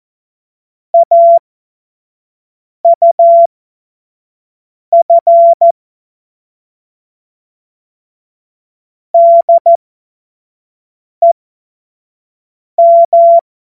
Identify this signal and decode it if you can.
Morse code, heard as a keyed tone, about 14 wpm, spaced as for 5 wpm, 680 Hz. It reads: AUF DEM
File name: Morsekurs / Witze / Witz 24 Juni 2025 25 F70